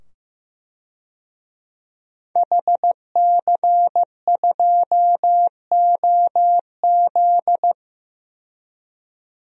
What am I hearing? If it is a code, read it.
Morse code, heard as a keyed tone, about 15 wpm, 700 Hz. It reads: HC2OZ
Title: DXCC-100 Call Signs - 5 Characters - 15wpm